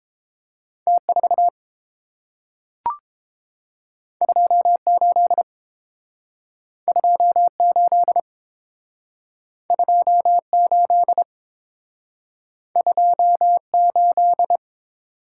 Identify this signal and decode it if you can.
Morse code, heard as a keyed tone, about 30 wpm, 700 Hz. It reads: T4 28 28 28 28